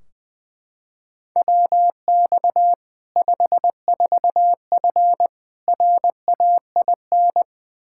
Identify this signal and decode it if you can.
Morse code, heard as a keyed tone, about 21 wpm, 700 Hz. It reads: WX 54F RAIN
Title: Practice QSO - Exchange 2 - Info - 20wpm